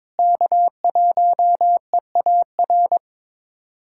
Morse code, heard as a keyed tone, about 22 wpm, 700 Hz. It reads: K1EAR
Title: USA Callsign DE Callsign 22wpm